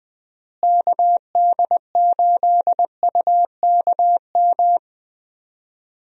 Morse code, heard as a keyed tone, about 20 wpm, 700 Hz. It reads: KD8UKM
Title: USA Call Signs - 6 Characters - Repeated 2x - 20wpm